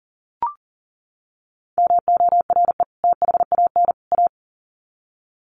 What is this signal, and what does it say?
Morse code, heard as a keyed tone, about 40 wpm, 700 Hz. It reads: MORE THAN A
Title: Sets of 3 Words 40wpm